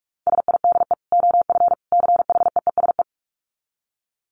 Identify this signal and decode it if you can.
Morse code, heard as a keyed tone, about 44 wpm, 700 Hz. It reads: SIDE OF CHEESE
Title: Dit Challenge 45wpm